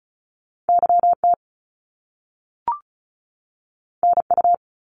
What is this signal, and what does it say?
Morse code, heard as a keyed tone, about 36 wpm, 700 Hz. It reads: YT NU
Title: Canadian Provinces and Territories 35wpm